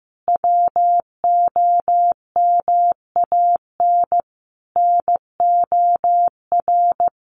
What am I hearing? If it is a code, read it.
Morse code, heard as a keyed tone, about 15 wpm, 700 Hz. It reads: WOMAN NOR
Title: Sentences from Top 1500 Words 15wpm